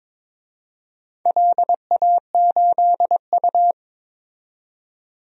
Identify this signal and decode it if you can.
Morse code, heard as a keyed tone, about 21 wpm, 700 Hz. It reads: LA8U
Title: DXCC-100 Call Signs - 4 Characters - 22wpm